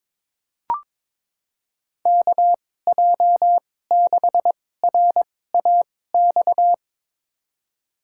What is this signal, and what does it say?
Morse code, heard as a keyed tone, about 22 wpm, spaced as for 16 wpm, 700 Hz. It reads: KJ6RAX
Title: USA Call Signs - 2x Character Spacing - 22wpm